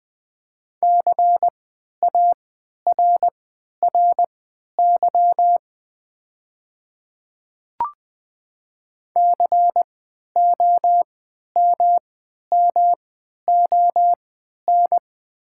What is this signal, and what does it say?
Morse code, heard as a keyed tone, about 20 wpm, spaced as for 11 wpm, 700 Hz. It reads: CARRY COMMON
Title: 3x Character Spacing - Top 500 Words - 20wpm